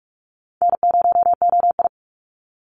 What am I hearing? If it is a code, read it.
Morse code, heard as a keyed tone, about 46 wpm, 690 Hz. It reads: N0OI
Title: NA SOTA Chaser Callsigns 45wpm